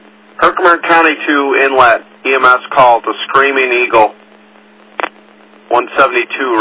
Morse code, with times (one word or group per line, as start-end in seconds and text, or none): none